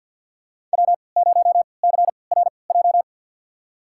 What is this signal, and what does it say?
Morse code, heard as a keyed tone, about 52 wpm, spaced as for 29 wpm, 700 Hz. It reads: W0CRJ
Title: USA Call Signs - 5 Characters - 3x Character Spacing - 50wpm